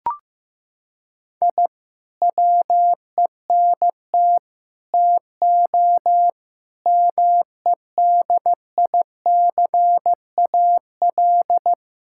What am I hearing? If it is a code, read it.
Morse code, heard as a keyed tone, about 15 wpm, 700 Hz. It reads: I WENT TO MEDICAL
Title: Sentences from Top 1000 Words - Review Part 5 - 15wpm